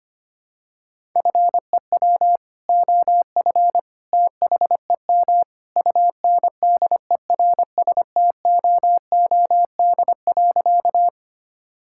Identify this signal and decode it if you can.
Morse code, heard as a keyed tone, about 25 wpm, 700 Hz. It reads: FEW OF THEM UNDERSTOOD.
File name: practice / 25wpm / Sentences with 2-Word Phrases - 25wpm